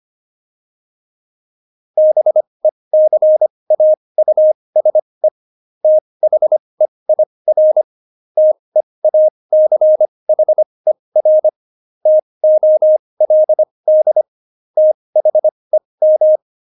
Morse code, heard as a keyed tone, about 25 wpm, spaced as for 20 wpm, 620 Hz. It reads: BECAUSE THEIR TEACHER TOLD THEM
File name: Jokes-25-char-20-eff